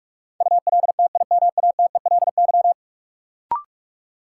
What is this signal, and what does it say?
Morse code, heard as a keyed tone, about 41 wpm, 700 Hz. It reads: ULTIMATELY
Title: Sentences from Top 1200 Words - New Word Drill Alt - 45wpm